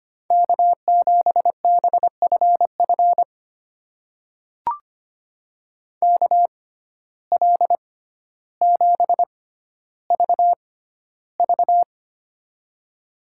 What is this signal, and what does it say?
Morse code, heard as a keyed tone, about 25 wpm, spaced as for 13 wpm, 700 Hz. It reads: K7BFF KL7VV
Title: USA Call Signs - 5 Characters - 6x Character Spacing - 25wpm